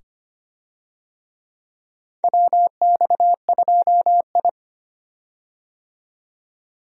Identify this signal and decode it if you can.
Morse code, heard as a keyed tone, about 25 wpm, 700 Hz. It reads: WX2I
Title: USA Call Signs - 4 Characters - 25wpm